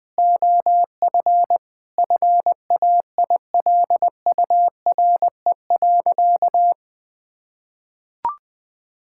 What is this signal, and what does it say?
Morse code, heard as a keyed tone, about 20 wpm, 700 Hz. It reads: OF FAILURE.
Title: Sentences from Top 1000 Words - Review Part 6 - 20wpm